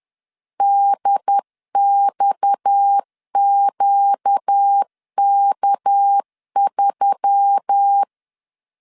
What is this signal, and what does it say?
Morse code, heard as a keyed tone, about 10 wpm, 800 Hz. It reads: DXQK3